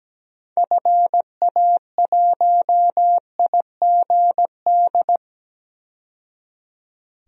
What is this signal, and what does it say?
Morse code, heard as a keyed tone, about 17 wpm, 700 Hz. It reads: FA1IGD
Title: DXCC-100 Call Signs - 6 Characters - 17wpm